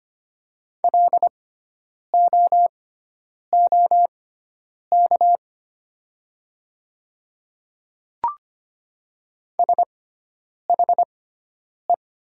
Morse code, heard as a keyed tone, about 25 wpm, spaced as for 9 wpm, 700 Hz. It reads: LOOK SHE
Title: 6x Character Spacing - Top 100 Words - 25wpm